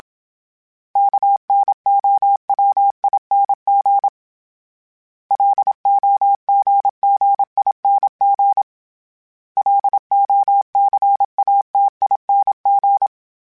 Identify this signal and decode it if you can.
Morse code, heard as a keyed tone, about 27 wpm, 800 Hz. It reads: KNOWING LOGGING LOCATING
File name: Special INGs-mp3s / Special INGs-mp3s / Sngl Ing @25